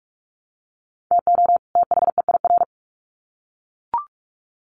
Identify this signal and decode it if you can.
Morse code, heard as a keyed tone, about 44 wpm, 700 Hz. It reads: TO THEIR
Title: Sets of 2 Words 45wpm